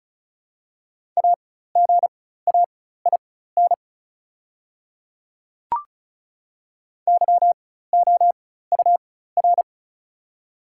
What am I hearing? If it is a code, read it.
Morse code, heard as a keyed tone, about 36 wpm, spaced as for 16 wpm, 700 Hz. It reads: AGAIN YOUR
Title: 4x Character Spacing - Top 200 Words - 35wpm